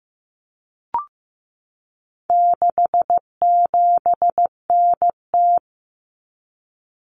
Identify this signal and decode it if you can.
Morse code, heard as a keyed tone, about 15 wpm, 700 Hz. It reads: 67NT
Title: Common Cut Numbers - 4-Digit 15wpm